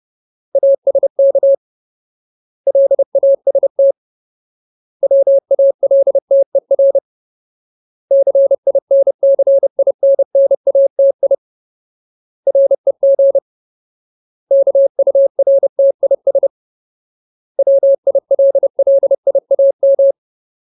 Morse code, heard as a keyed tone, about 30 wpm, 550 Hz. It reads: ASK LAST WALTER CINCINNATI REG KURTIS WILLIAM
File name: Ultimate Word List / 30WPM Ultimate